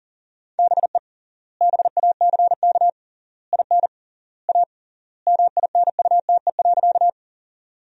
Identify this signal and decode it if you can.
Morse code, heard as a keyed tone, about 40 wpm, 700 Hz. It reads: BE BACK IN A MINUTE.
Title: 3x Word-Spacing - Sentences from Top 500 Words - 40wpm